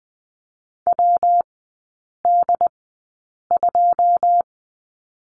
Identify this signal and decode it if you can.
Morse code, heard as a keyed tone, about 19 wpm, spaced as for 8 wpm, 700 Hz. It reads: WD2
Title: 133 Lesson 22e - Question Mark - Three Characters 20wpm